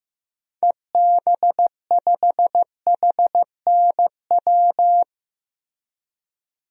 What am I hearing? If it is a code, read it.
Morse code, heard as a keyed tone, about 16 wpm, 700 Hz. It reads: EB5HNW
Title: DXCC-100 Call Signs - 6 Characters - Repeated 2x - 15wpm